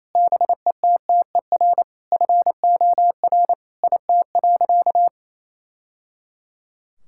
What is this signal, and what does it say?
Morse code, heard as a keyed tone, about 27 wpm, 700 Hz. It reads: BETTER FOR IT.
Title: Sentences from Top 1000 Words - Review Part 5 - 28wpm